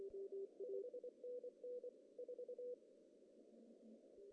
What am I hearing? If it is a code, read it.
Morse code, heard as a keyed tone, about 24 wpm, 490 Hz. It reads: 5NN 4
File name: CW / cqww-cw-k5tr